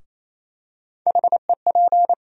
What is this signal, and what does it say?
Morse code, heard as a keyed tone, about 28 wpm, 700 Hz. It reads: HEP
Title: USA Call Sign Postfixes - 3 Letters 28wpm